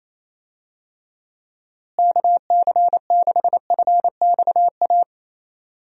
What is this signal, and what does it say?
Morse code, heard as a keyed tone, about 28 wpm, 700 Hz. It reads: KC6FXA
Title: USA Call Signs - 6 Characters - Repeated 2x - 28wpm